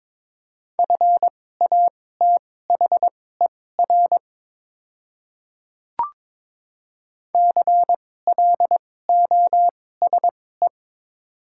Morse code, heard as a keyed tone, about 22 wpm, spaced as for 16 wpm, 700 Hz. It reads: FATHER CLOSE
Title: 2x Character Spacing - Top 300 Words - 22wpm